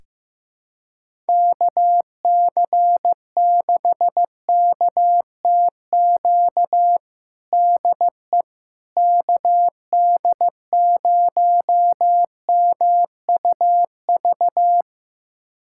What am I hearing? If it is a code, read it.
Morse code, heard as a keyed tone, about 16 wpm, 700 Hz. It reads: KC6KTQ DE KD0MUV
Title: USA Callsign DE Callsign 15wpm